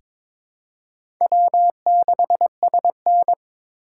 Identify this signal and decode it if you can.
Morse code, heard as a keyed tone, about 21 wpm, 700 Hz. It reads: W6SN
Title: USA Call Signs - 3x Character Spacing - 22wpm